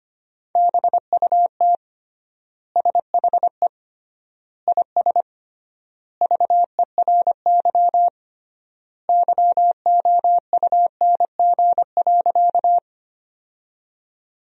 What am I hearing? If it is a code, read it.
Morse code, heard as a keyed tone, about 25 wpm, 700 Hz. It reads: BUT SHE IS VERY YOUNG.
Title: 3x Word-Spacing - Sentences from Top 400 Words - 25wpm